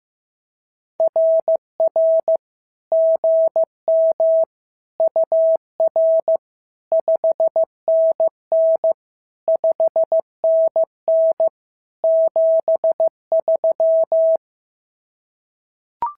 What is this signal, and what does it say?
Morse code, heard as a keyed tone, about 15 wpm, 650 Hz. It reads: RR GM UR 5NN 5NN 73
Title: SOTA QSO Practice 15wpm